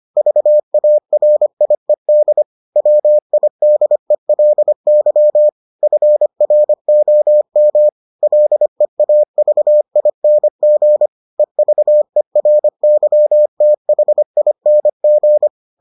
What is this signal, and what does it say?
Morse code, heard as a keyed tone, about 25 wpm, 600 Hz. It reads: VARIED WIDELY FROM LEAVING EVERYTHING